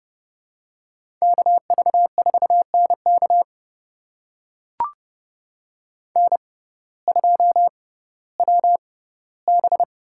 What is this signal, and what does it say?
Morse code, heard as a keyed tone, about 31 wpm, spaced as for 17 wpm, 700 Hz. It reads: KV4NK N2WB
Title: USA Call Signs - 5 Characters - 6x Character Spacing - 30wpm